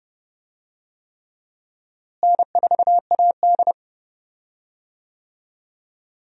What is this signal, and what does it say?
Morse code, heard as a keyed tone, about 29 wpm, 700 Hz. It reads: N4AD
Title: USA Call Signs - 4 Characters - Repeated 3x - 30wpm